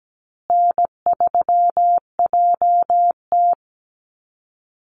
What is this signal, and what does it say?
Morse code, heard as a keyed tone, about 17 wpm, 700 Hz. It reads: N3JT
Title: Call Signs Repeated 2x - 17wpm